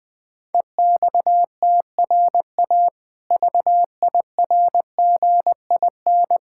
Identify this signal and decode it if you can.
Morse code, heard as a keyed tone, about 20 wpm, 700 Hz. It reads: EXTRA VIRGIN
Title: Difficult Words with Hints - 20wpm